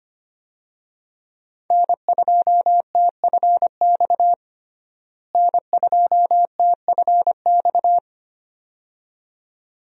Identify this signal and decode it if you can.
Morse code, heard as a keyed tone, about 25 wpm, 700 Hz. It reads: N2TFX N2TFX
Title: USA Call Signs - 5 Characters - Repeated 2x - 25wpm